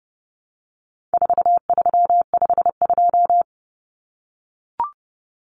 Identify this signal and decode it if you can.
Morse code, heard as a keyed tone, about 30 wpm, 700 Hz. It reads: 4352
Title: Common Cut Numbers - 4-Digit 30wpm